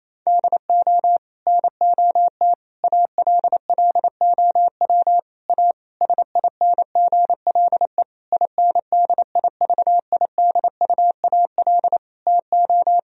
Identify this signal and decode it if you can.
Morse code, heard as a keyed tone, about 28 wpm, 700 Hz. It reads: DO NOT ALLOW A SINGLE INDIVIDUAL TO
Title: Sentences from Top 1000 Words - Review Part 4 - 28wpm